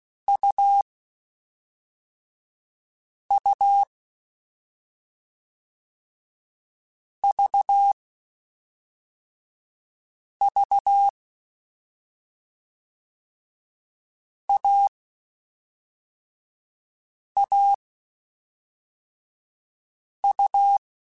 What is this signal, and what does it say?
Morse code, heard as a keyed tone, about 16 wpm, 800 Hz. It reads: U U V V A A U
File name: Short Runs / Short Runs-mp3s / AUV AUV4 @5